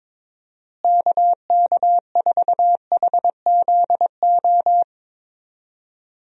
Morse code, heard as a keyed tone, about 22 wpm, 700 Hz. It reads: KK4HZO
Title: USA Call Signs - 6 Characters - Repeated 2x - 22wpm